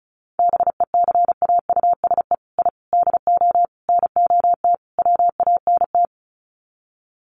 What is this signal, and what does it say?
Morse code, heard as a keyed tone, about 34 wpm, 700 Hz. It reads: BECAUSE I DO NOT WANT
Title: Sets of 5 Words Encore 35wpm